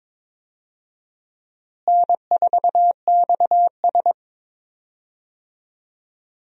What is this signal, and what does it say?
Morse code, heard as a keyed tone, about 22 wpm, 700 Hz. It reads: N4XS
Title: USA Call Signs - 4 Characters - Repeated 2x - 22wpm